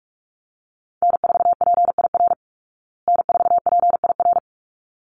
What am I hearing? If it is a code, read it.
Morse code, heard as a keyed tone, about 44 wpm, 700 Hz. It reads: N4PIR N4PIR
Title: Call Signs Repeated 3x - 45wpm